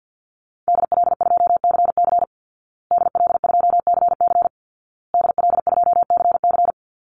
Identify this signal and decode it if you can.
Morse code, heard as a keyed tone, about 50 wpm, 700 Hz. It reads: DL2CC DL2CC DL2CC
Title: Call Signs Repeated 2x - 50wpm